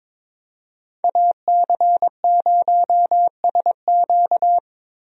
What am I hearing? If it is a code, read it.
Morse code, heard as a keyed tone, about 22 wpm, 700 Hz. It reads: AC0SQ
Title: USA Call Signs - 5 Characters - Repeated 3x - 22wpm